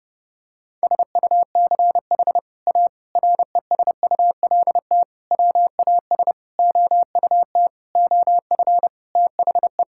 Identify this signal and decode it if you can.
Morse code, heard as a keyed tone, about 30 wpm, 700 Hz. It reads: SUCH A RESULT WAS OUT OF THE
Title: Sentences from Top 1000 Words - Review Part 5 - 30wpm